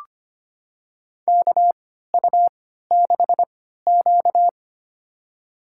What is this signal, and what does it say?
Morse code, heard as a keyed tone, about 25 wpm, spaced as for 14 wpm, 700 Hz. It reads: KU6Q
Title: USA Call Signs - 3x Character Spacing - 25wpm